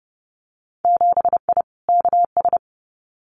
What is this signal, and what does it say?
Morse code, heard as a keyed tone, about 30 wpm, 700 Hz. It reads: 7I KS
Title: Winter Field Day Exchanges Encore 30wpm